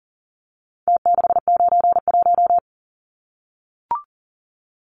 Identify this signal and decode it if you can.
Morse code, heard as a keyed tone, about 40 wpm, 700 Hz. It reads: T691
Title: Numbers - Contest Style - 4-Digit 40wpm